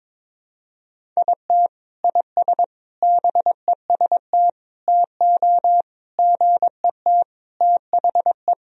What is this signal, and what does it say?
Morse code, heard as a keyed tone, about 22 wpm, 700 Hz. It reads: IT IS BEST TO GET THE
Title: Sentences from Top 1000 Words - Review Part 1 - 22wpm